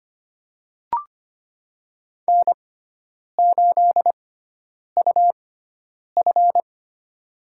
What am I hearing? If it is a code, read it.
Morse code, heard as a keyed tone, about 25 wpm, spaced as for 9 wpm, 700 Hz. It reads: N8UF
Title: USA Call Signs - 4 Characters - 6x Character Spacing - 25wpm